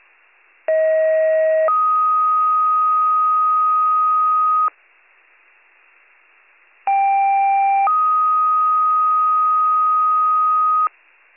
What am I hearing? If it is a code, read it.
Two-tone page, heard as a one-second tone, then a three-second tone, about 640 then 1250 Hz, about 790 then 1250 Hz. It is heard from 0.7 to 10.9 s.